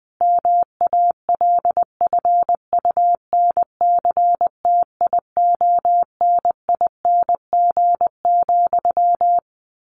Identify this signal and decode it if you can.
Morse code, heard as a keyed tone, about 20 wpm, 700 Hz. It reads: MALFUNCTIONING,
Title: A Sound Of Thunder 20wpm